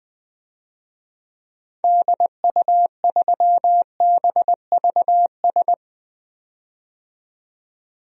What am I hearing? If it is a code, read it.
Morse code, heard as a keyed tone, about 20 wpm, 700 Hz. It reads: DU3BVS
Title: DXCC-100 Call Signs - 6 Characters - 20wpm